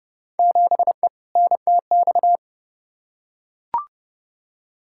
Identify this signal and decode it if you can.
Morse code, heard as a keyed tone, about 31 wpm, 700 Hz. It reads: 7E NTX
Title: Field Day Exchanges Encore - 30wpm